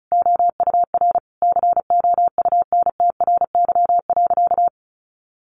Morse code, heard as a keyed tone, about 34 wpm, 700 Hz. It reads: OUR COUNTRY.
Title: Sentences from Top 1300 Words 35wpm